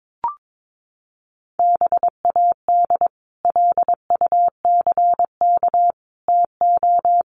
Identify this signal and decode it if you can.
Morse code, heard as a keyed tone, about 22 wpm, 700 Hz. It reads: BAD LUCK TO
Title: Sentences from Top 1500 Words 22wpm